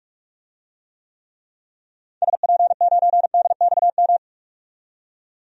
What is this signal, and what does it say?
Morse code, heard as a keyed tone, about 46 wpm, 700 Hz. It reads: SP9DXM
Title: DXCC-100 Call Signs - 6 Characters - 45wpm